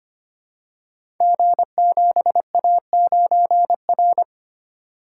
Morse code, heard as a keyed tone, about 25 wpm, 700 Hz. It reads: G7A9R
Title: Code Groups - 5 Letter:Numbers - 25wpm